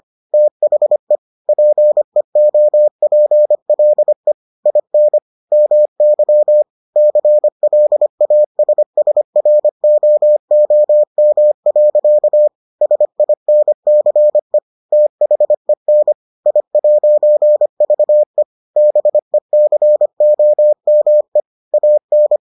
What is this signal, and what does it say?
Morse code, heard as a keyed tone, about 25 wpm, 600 Hz. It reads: THE PEOPLE IN MY CLASSROOM. SINCE THEN I'VE BECOME AN